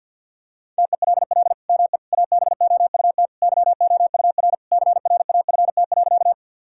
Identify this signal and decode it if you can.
Morse code, heard as a keyed tone, about 46 wpm, 700 Hz. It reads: TELL ME ABOUT YOUR CRAFT.
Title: Sentences from Top 1000 Words - Review Part 6 - 50wpm